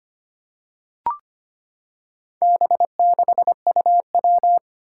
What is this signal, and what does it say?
Morse code, heard as a keyed tone, about 25 wpm, 700 Hz. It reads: B6UW
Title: Four Challenging Letter-Number Combinations 25wpm